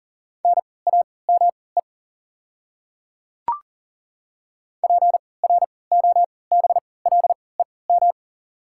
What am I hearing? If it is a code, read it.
Morse code, heard as a keyed tone, about 40 wpm, spaced as for 23 wpm, 690 Hz. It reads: NAME PROBLEM
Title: 3x Character Spacing - Top 500 Words - 40wpm